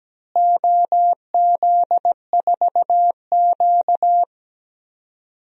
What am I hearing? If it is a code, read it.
Morse code, heard as a keyed tone, about 17 wpm, 700 Hz. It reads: OZ4Q
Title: DXCC-100 Call Signs - 4 Characters - Repeated 3x - 17wpm